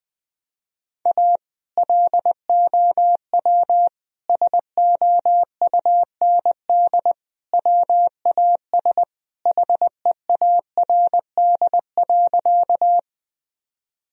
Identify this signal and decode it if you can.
Morse code, heard as a keyed tone, about 20 wpm, 700 Hz. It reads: A LOW SOUND WAS HEARD.
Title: Sentences from Top 1000 Words - Review Part 1 - 20wpm